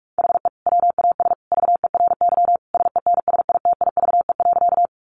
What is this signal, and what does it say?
Morse code, heard as a keyed tone, about 44 wpm, 700 Hz. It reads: HE WAS VERY SENSITIVE.
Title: Sentences from Top 1400 Words 45wpm